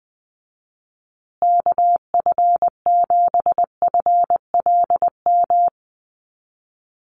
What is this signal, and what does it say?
Morse code, heard as a keyed tone, about 20 wpm, 700 Hz. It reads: KF7FLM
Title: USA Call Signs Repeated 3x 20wpm